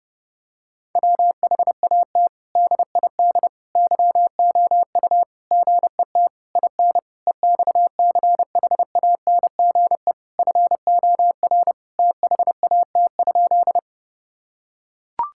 Morse code, heard as a keyed tone, about 30 wpm, 700 Hz. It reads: WHAT DID YOU GET IN EXCHANGE FOR THAT?
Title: Sentences from Top 1000 Words - Review Part 6 - 30wpm